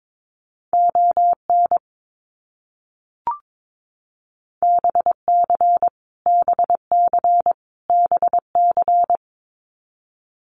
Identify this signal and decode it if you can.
Morse code, heard as a keyed tone, about 22 wpm, 700 Hz. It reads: ON BC BC BC
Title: Canadian Provinces and Territories - Repeated 3x - 22wpm